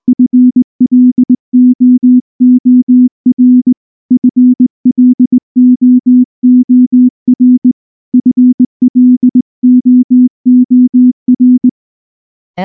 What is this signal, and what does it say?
Morse code, heard as a keyed tone, about 19 wpm, 250 Hz. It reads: FLOOR FLOOR FLOOR
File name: morse / morse-training-20WPM-3